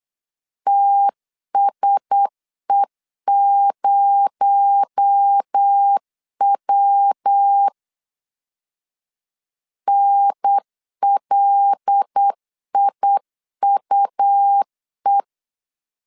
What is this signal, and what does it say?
Morse code, heard as a keyed tone, about 8 wpm, 800 Hz. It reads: TSE0W NLIUE
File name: LEZIONI / 6 PKX0 / 6 PKX0 8wpm 3341